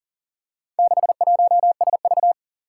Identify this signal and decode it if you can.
Morse code, heard as a keyed tone, about 40 wpm, 700 Hz. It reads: 61SV
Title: Four Challenging Letter-Number Combinations 40wpm